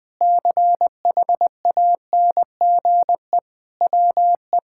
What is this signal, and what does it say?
Morse code, heard as a keyed tone, about 21 wpm, 700 Hz. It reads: CHANGE WE
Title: Sentences with 8-Word Phrases - 20wpm